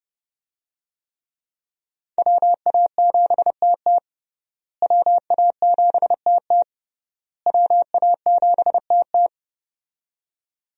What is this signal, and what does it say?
Morse code, heard as a keyed tone, about 30 wpm, 700 Hz. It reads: WA7TT WA7TT WA7TT
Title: USA Call Signs - 5 Characters - Repeated 3x - 30wpm